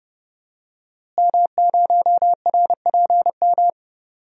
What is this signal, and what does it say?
Morse code, heard as a keyed tone, about 30 wpm, 700 Hz. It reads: M0RPM
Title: DXCC-100 Call Signs - 5 Characters - Repeated 3x - 30wpm